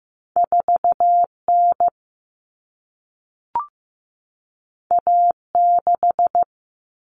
Morse code, heard as a keyed tone, about 15 wpm, 700 Hz. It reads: 4N A6
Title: Common Cut Numbers - 2-Digit 15wpm